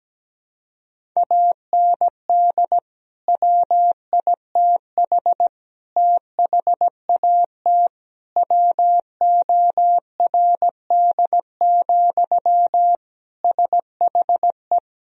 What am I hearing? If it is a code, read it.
Morse code, heard as a keyed tone, about 17 wpm, 700 Hz. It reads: AND WITH THAT WORD, SHE
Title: Sentences from Top 1000 Words - Review Part 1 - 17wpm